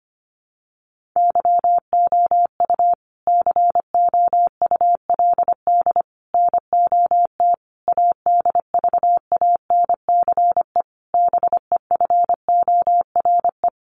Morse code, heard as a keyed tone, about 25 wpm, 700 Hz. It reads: YOU COULD NOT ADVANCE BEFORE
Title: Sentences from Top 1400 Words 25wpm